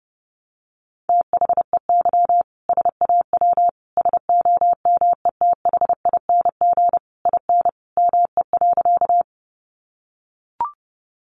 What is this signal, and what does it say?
Morse code, heard as a keyed tone, about 30 wpm, 700 Hz. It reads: THEY SAW SOMETHING IN ME.
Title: Sentences from Top 600 Words 30wpm